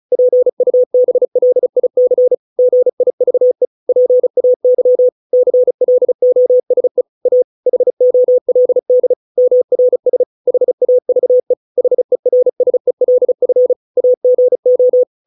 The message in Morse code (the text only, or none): PUBLIC GIVE PAY CLOSE A HOLD MRS HAVE HERSELF AGO